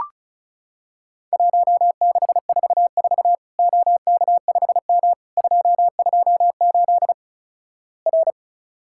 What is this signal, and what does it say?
Morse code, heard as a keyed tone, about 36 wpm, 700 Hz. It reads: 1644 OK5M 228 R
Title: Worked All Europe DX Contest - QTC Exchange - 35wpm